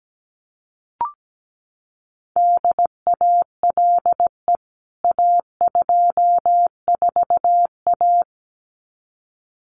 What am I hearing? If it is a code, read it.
Morse code, heard as a keyed tone, about 17 wpm, 700 Hz. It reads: DALE A24A
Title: CWT Exchange 17wpm